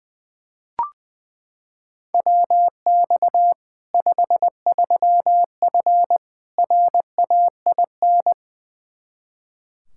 Morse code, heard as a keyed tone, about 20 wpm, 700 Hz. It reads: WX 53F RAIN
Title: Practice QSO - Exchange 2 - Info - 20wpm